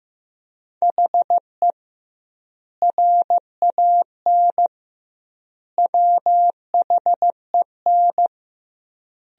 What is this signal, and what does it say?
Morse code, heard as a keyed tone, about 15 wpm, 700 Hz. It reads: HE RAN WHEN
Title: 2x Word-Spacing - Sentences from Top 500 Words - 15wpm